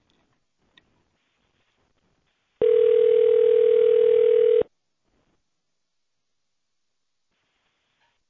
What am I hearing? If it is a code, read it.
Ringback tone (North America), 440 + 480 Hz.